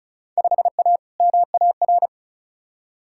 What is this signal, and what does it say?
Morse code, heard as a keyed tone, about 36 wpm, 700 Hz. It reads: 5A MAR
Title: Field Day Exchanges Encore - 35wpm